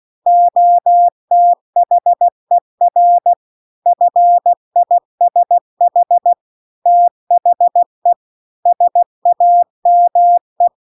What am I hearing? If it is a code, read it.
Morse code, heard as a keyed tone, about 16 wpm, 700 Hz. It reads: OTHER FISH THE SAME
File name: CW-The-Big-Twoheaded-River-2-16WPM